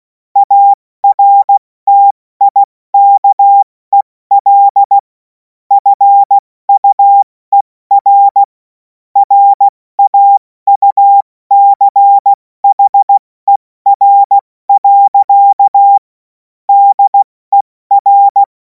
Morse code, heard as a keyed tone, about 16 wpm, 800 Hz. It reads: ARTIKEL FUER RAUCHER. DER
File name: Morsekurs / Witze / Witz 16. Juni 2026 70 F80